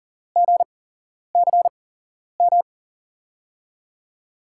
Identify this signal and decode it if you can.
Morse code, heard as a keyed tone, about 38 wpm, spaced as for 11 wpm, 700 Hz. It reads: GCM
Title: USA Call Signs - 5 Characters - 8x Character Spacing - 40wpm